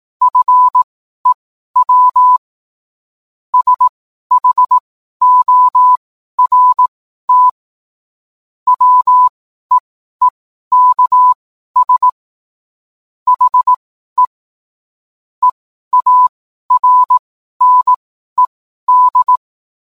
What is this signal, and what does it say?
Morse code, heard as a keyed tone, about 18 wpm, spaced as for 12 wpm, 1000 Hz. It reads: FEW SHORT WEEKS HE EARNED